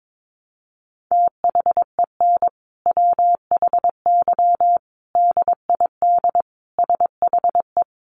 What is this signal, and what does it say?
Morse code, heard as a keyed tone, about 22 wpm, 700 Hz. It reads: THEN WHY DID SHE
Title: Sentences from Top 800 Words 22wpm